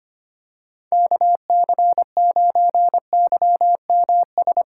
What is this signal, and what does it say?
Morse code, heard as a keyed tone, about 25 wpm, 700 Hz. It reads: KC9YMS
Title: USA Call Signs - 6 Characters - 8x Character Spacing - 25wpm